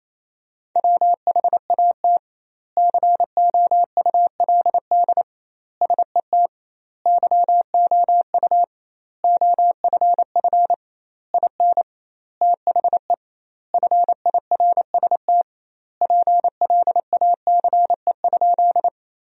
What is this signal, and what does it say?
Morse code, heard as a keyed tone, about 28 wpm, 700 Hz. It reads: WHAT COULD SET YOU OFF IN THE FIRST PLACE?
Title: 2x Word-Spacing - Sentences from Top 200 Words - 28wpm